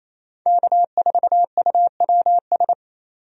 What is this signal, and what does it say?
Morse code, heard as a keyed tone, about 27 wpm, 700 Hz. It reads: K4UWS
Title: USA Call Signs - 5 Characters - 28wpm